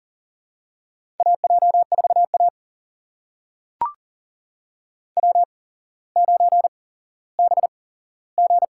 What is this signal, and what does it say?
Morse code, heard as a keyed tone, about 39 wpm, spaced as for 17 wpm, 700 Hz. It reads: AJ4A W9BG
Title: USA Call Signs - 4 Characters - 8x Character Spacing - 40wpm